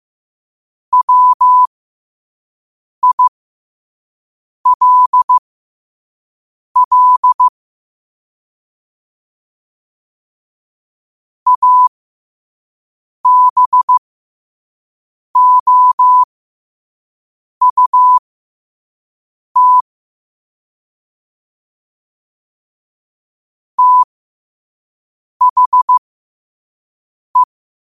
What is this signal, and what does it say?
Morse code, heard as a keyed tone, about 15 wpm, spaced as for 5 wpm, 1000 Hz. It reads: WILL ABOUT THE
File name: qotdfiles / 5wpm